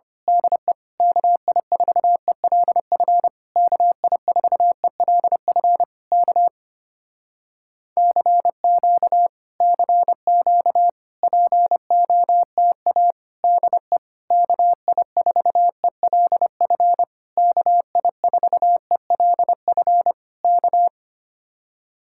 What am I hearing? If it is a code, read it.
Morse code, heard as a keyed tone, about 25 wpm, 700 Hz. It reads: DE KI4ELF KI4ELF K CQ CQ POTA DE KI4ELF KI4ELF K
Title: Speed-Racing Common QSO Phrases - 25wpm